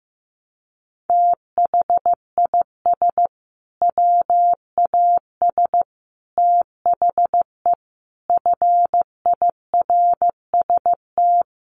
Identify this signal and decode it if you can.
Morse code, heard as a keyed tone, about 15 wpm, 700 Hz. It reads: THIS WAS THE FIRST
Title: Sets of 5 Words Encore 15wpm